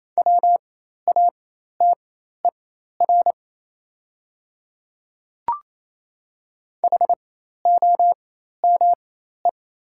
Morse code, heard as a keyed tone, about 27 wpm, spaced as for 13 wpm, 700 Hz. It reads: WATER HOME
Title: 4x Character Spacing - Top 300 Words - 28wpm